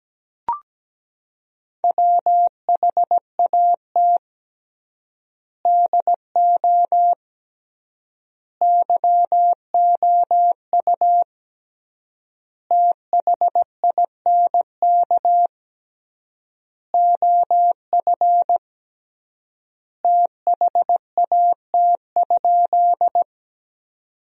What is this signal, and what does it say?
Morse code, heard as a keyed tone, about 17 wpm, 700 Hz. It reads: WHAT DO YOU THINK OF THAT?
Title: 3x Word-Spacing - Sentences from Top 200 Words - 17wpm